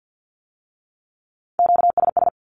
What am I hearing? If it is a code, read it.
Morse code, heard as a keyed tone, about 50 wpm, 700 Hz. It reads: QSS
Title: Q-Codes with Spoken Meaning - 50wpm